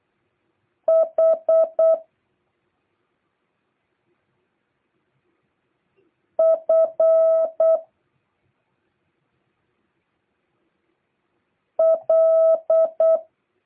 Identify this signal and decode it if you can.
Morse code, heard as a keyed tone, about 8 wpm, 650 Hz. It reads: H F L